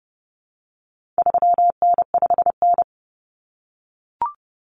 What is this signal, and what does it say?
Morse code, heard as a keyed tone, about 30 wpm, 700 Hz. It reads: 3N5N
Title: Common Cut Numbers - 4-Digit 30wpm